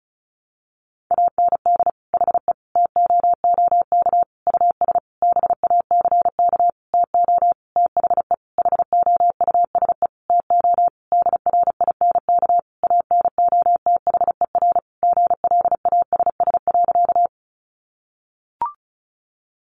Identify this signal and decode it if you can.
Morse code, heard as a keyed tone, about 34 wpm, 700 Hz. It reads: AND HE TOOK US BACK TO THE HOUSE TO DRINK ANOTHER GLASS.
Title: Sentences from Top 1500 Words 35wpm